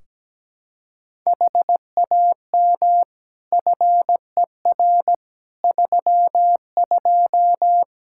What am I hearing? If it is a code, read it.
Morse code, heard as a keyed tone, about 17 wpm, 700 Hz. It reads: HAM FER 32
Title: K1USN SST QSO Practice 17wpm